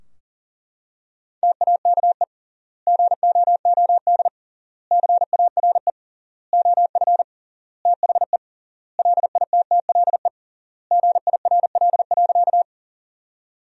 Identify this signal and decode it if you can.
Morse code, heard as a keyed tone, about 40 wpm, 700 Hz. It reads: TAKE GOOD CARE OF THE LITTLE GIRL.
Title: 3x Word-Spacing - Sentences from Top 400 Words - 40wpm